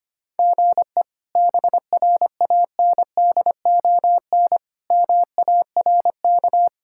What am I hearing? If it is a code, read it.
Morse code, heard as a keyed tone, about 25 wpm, 700 Hz. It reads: GE BRANDON MARK
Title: K1USN SST QSO Practice - 25wpm